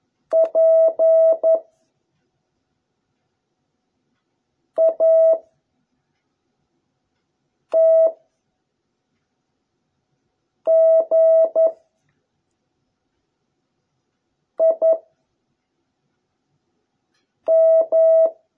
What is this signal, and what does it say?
Morse code, heard as a keyed tone, about 11 wpm, 660 Hz. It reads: P A T G I M